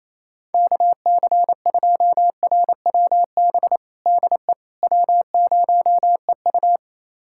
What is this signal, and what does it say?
Morse code, heard as a keyed tone, about 27 wpm, 700 Hz. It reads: KC2RWB DE W0EU
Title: USA Callsign DE Callsign 28wpm